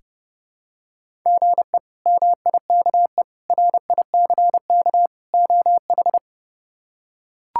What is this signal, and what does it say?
Morse code, heard as a keyed tone, about 30 wpm, 700 Hz. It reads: GE MIKE RICK OH
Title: K1USN SST QSO Practice - 30wpm